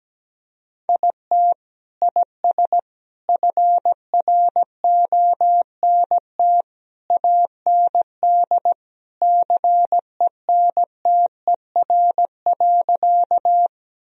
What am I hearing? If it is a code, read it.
Morse code, heard as a keyed tone, about 17 wpm, 700 Hz. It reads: IT IS FRONT AND CENTER.
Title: Sentences from Top 1000 Words - Review Part 2 - 17wpm